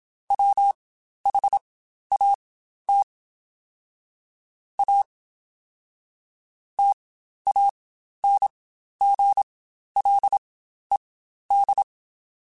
Morse code, heard as a keyed tone, about 27 wpm, spaced as for 12 wpm, 800 Hz. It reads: WHAT A TANGLED
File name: Quotes Spaced / Quotes Spaced-mp3s / Deceive sp @25